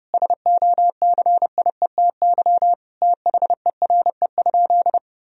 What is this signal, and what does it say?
Morse code, heard as a keyed tone, about 32 wpm, 700 Hz. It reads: SOCIETY THERE?
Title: Sentences from Top 1000 Words - Review Part 4 - 30wpm